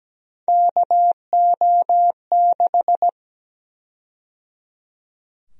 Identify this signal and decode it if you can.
Morse code, heard as a keyed tone, about 17 wpm, 700 Hz. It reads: KO6
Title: USA Call Sign Prefixes - 17wpm